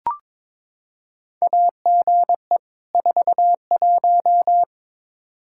Morse code, heard as a keyed tone, about 22 wpm, 700 Hz. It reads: AGE 41
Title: Practice QSO - Exchange 3 - More Info - 22wpm